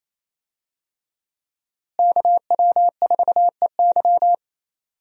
Morse code, heard as a keyed tone, about 28 wpm, 700 Hz. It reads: KW4EY
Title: USA Call Signs - 5 Characters - Repeated 3x - 28wpm